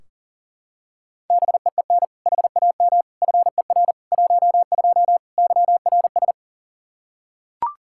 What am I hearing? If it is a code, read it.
Morse code, heard as a keyed tone, about 37 wpm, 690 Hz. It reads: BEEN HAM FER 12 YRS
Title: Practice QSO - Exchange 3 - More Info - 40wpm